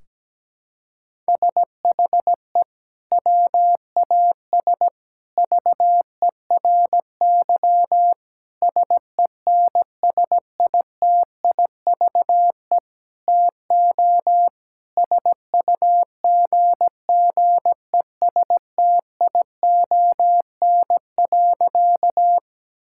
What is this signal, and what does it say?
Morse code, heard as a keyed tone, about 17 wpm, 700 Hz. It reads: SHE WAS VERY SENSITIVE TO SUGGESTION.
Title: Sentences from Top 1600 Words 17wpm